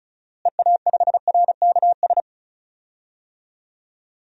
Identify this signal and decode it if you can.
Morse code, heard as a keyed tone, about 36 wpm, 700 Hz. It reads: EA5RKS
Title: DXCC-100 Call Signs - 6 Characters - Repeated 2x - 35wpm